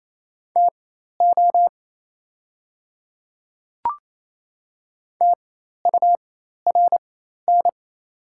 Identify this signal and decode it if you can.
Morse code, heard as a keyed tone, about 28 wpm, spaced as for 13 wpm, 690 Hz. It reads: TO TURN
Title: 4x Character Spacing - Top 200 Words - 28wpm